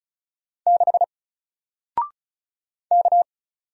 Morse code, heard as a keyed tone, about 36 wpm, 700 Hz. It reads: 6 K
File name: Single-Letter-Number-35wpm